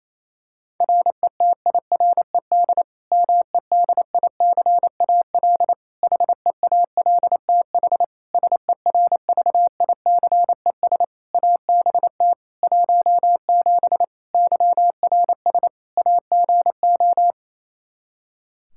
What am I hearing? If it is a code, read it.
Morse code, heard as a keyed tone, about 27 wpm, 700 Hz. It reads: RETIRED MEDICAL HEALTH SERVICES ABT 17 YRS AGO
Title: K1USN SST QSO Practice 28wpm